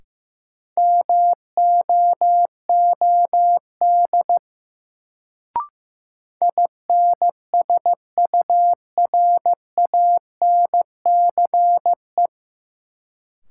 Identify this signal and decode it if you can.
Morse code, heard as a keyed tone, about 15 wpm, 700 Hz. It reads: MOOD INSURANCE
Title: Sentences from Top 1100 Words - New Word Drill Alt - 15wpm